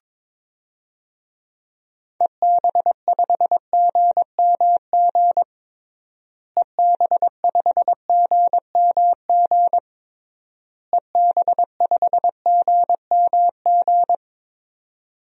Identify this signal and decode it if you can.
Morse code, heard as a keyed tone, about 21 wpm, 700 Hz. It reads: EB5GMG EB5GMG EB5GMG
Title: DXCC-100 Call Signs - 6 Characters - Repeated 3x - 22wpm